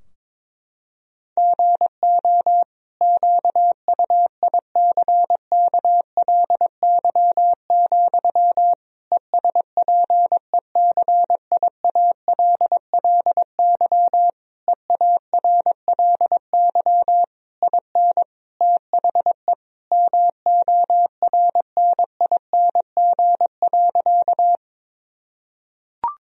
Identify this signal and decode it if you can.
Morse code, heard as a keyed tone, about 22 wpm, 700 Hz. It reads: GO QUICKLY, ESPECIALLY EARLY IN THE MORNING.
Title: Sentences from Top 1000 Words - Review Part 4 - 22wpm